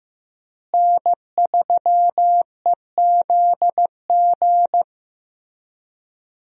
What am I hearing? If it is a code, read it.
Morse code, heard as a keyed tone, about 16 wpm, 700 Hz. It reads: N3EZG
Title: USA Call Signs - 5 Characters - Repeated 3x - 15wpm